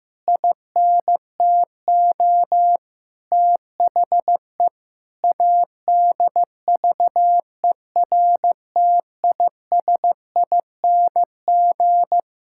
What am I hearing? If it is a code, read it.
Morse code, heard as a keyed tone, about 15 wpm, 700 Hz. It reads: INTO THE ADVERTISING
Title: Sentences from Top 1000 Words - Review Part 6 - 15wpm